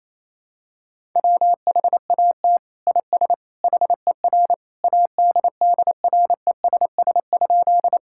Morse code, heard as a keyed tone, about 28 wpm, 700 Hz. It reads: WHAT IS HER ADDRESS?
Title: Sentences from Top 1000 Words - Review Part 6 - 28wpm